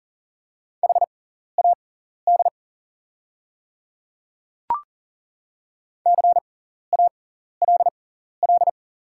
Morse code, heard as a keyed tone, about 40 wpm, spaced as for 14 wpm, 700 Hz. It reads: HAD CALL
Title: 6x Character Spacing - Top 500 Words - 40wpm